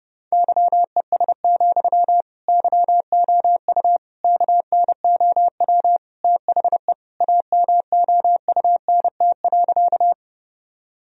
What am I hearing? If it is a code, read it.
Morse code, heard as a keyed tone, about 32 wpm, 700 Hz. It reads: YES, YOU KNOW THE AMOUNT.
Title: Sentences from Top 1000 Words - Review Part 4 - 30wpm